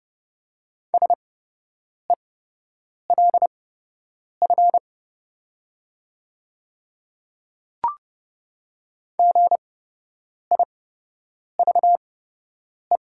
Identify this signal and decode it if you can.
Morse code, heard as a keyed tone, about 29 wpm, spaced as for 8 wpm, 700 Hz. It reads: SELF GIVE
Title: 8x Character Spacing - Top 200 Words - 30wpm